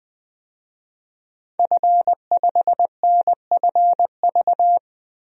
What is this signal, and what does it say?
Morse code, heard as a keyed tone, about 21 wpm, 700 Hz. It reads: F5NFV
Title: DXCC-100 Call Signs - 5 Characters - Repeated 3x - 20wpm